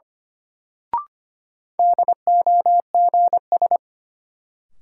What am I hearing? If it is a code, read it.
Morse code, heard as a keyed tone, about 25 wpm, 700 Hz. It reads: DOGS
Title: Sentences from Top 1600 Words - New Word Drill - 25wpm